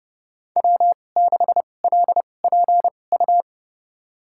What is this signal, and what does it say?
Morse code, heard as a keyed tone, about 32 wpm, spaced as for 22 wpm, 700 Hz. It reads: W6LPU
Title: USA Call Signs - 5 Characters - 2x Character Spacing - 30wpm